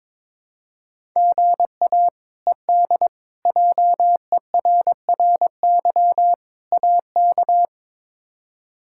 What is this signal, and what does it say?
Morse code, heard as a keyed tone, about 22 wpm, 700 Hz. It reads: GA ED JERRY AK
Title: K1USN SST QSO Practice - 22wpm